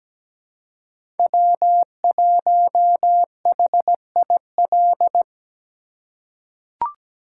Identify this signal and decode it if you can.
Morse code, heard as a keyed tone, about 17 wpm, 700 Hz. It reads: W1HIL
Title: USA Call Signs - 5 Characters - 3x Character Spacing - 17wpm